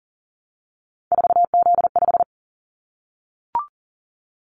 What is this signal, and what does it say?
Morse code, heard as a keyed tone, about 40 wpm, 700 Hz. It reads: 4Z5
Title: Callsign Prefixes for Top 50 DXCC - 40wpm